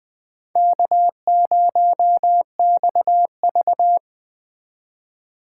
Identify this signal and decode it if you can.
Morse code, heard as a keyed tone, about 20 wpm, 700 Hz. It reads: K0XV